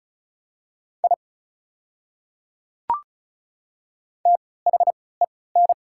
Morse code, heard as a keyed tone, about 36 wpm, spaced as for 20 wpm, 690 Hz. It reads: I THEN
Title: 3x Character Spacing - Top 100 Words - 35wpm